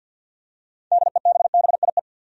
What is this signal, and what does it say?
Morse code, heard as a keyed tone, about 48 wpm, 700 Hz. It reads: DEBBIE
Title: USA Common Names - 50wpm